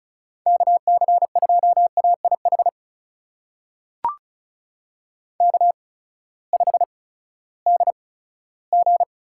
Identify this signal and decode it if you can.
Morse code, heard as a keyed tone, about 34 wpm, spaced as for 17 wpm, 700 Hz. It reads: KC2AIH K5DG
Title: USA Call Signs - 8x Character Spacing - 35wpm